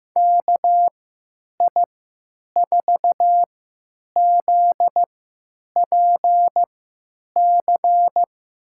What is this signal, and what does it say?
Morse code, heard as a keyed tone, about 15 wpm, spaced as for 9 wpm, 700 Hz. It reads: KI4ZPC
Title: USA Call Signs - 6 Characters - 3x Character Spacing - 15wpm